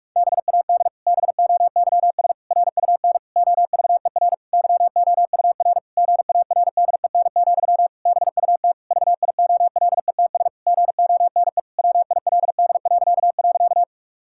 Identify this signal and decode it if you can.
Morse code, heard as a keyed tone, about 46 wpm, 700 Hz. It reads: BAD BOYS RUN OVER YOUR GARDEN, BUT VIOLETS GONE WILD..
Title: Pangram Challenge v2 - 45wpm